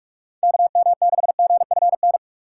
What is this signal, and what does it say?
Morse code, heard as a keyed tone, about 46 wpm, 700 Hz. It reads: KM6GFN
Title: USA Call Signs - 6 Characters - 2x Character Spacing - 45wpm